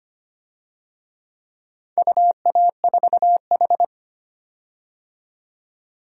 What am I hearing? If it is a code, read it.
Morse code, heard as a keyed tone, about 25 wpm, 700 Hz. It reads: UA4H
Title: DXCC-100 Call Signs - 4 Characters - 25wpm